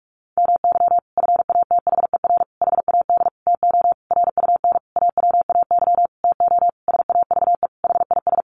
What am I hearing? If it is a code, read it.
Morse code, heard as a keyed tone, about 44 wpm, 700 Hz. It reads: MY FATHER HAD TO RUN AWAY TO SAVE HIS
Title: Sentences from Top 1100 Words 45wpm